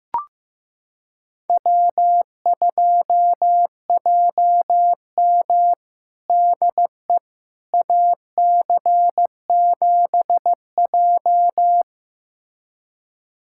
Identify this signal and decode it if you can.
Morse code, heard as a keyed tone, about 15 wpm, 700 Hz. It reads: W2JM DE AC7J
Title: USA Callsign DE Callsign 15wpm